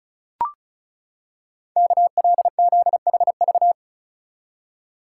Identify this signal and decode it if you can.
Morse code, heard as a keyed tone, about 36 wpm, 700 Hz. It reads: KLZHV
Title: Code Groups - 5 Letters - 35wpm